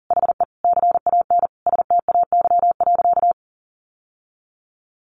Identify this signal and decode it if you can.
Morse code, heard as a keyed tone, about 37 wpm, 700 Hz. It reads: HE CAN STAY.
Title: Sentences from Top 1000 Words 40wpm